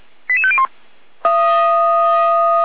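Five-tone selective call (ZVEI), heard as quick runs of five tones, tones of about 70 ms. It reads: 88501
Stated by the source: ZVEI five-tone selective call received on an SDR for a fire brigade.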